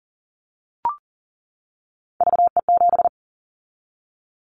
Morse code, heard as a keyed tone, about 40 wpm, 700 Hz. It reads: VE7
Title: Callsign Prefixes for Top 100 DXCC - 40wpm